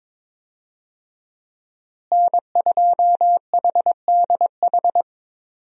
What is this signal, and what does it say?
Morse code, heard as a keyed tone, about 22 wpm, 700 Hz. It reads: N2HDH
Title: USA Call Signs - 5 Characters - Repeated 2x - 22wpm